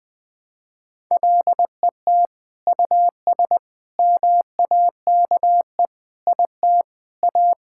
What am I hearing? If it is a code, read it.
Morse code, heard as a keyed tone, about 20 wpm, 700 Hz. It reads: LET US MAKE IT A
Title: Sentences from Top 1600 Words 20wpm